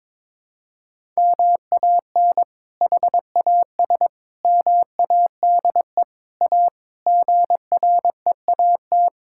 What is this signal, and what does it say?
Morse code, heard as a keyed tone, about 22 wpm, 700 Hz. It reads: MAN HAS MADE A GREAT
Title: Sentences from Top 1000 Words - Review Part 6 - 22wpm